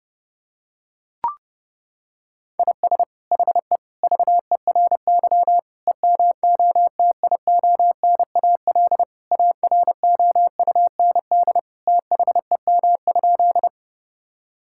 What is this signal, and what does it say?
Morse code, heard as a keyed tone, about 32 wpm, 700 Hz. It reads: IS HE VERY EMOTIONAL AROUND THEM?
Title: Sentences from Top 1000 Words - Review Part 5 - 30wpm